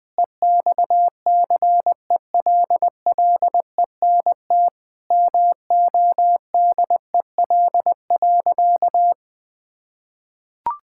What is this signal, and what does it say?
Morse code, heard as a keyed tone, about 20 wpm, 700 Hz. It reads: EXCELLENT MODEL.
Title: Sentences from Top 1000 Words - Review Part 5 - 20wpm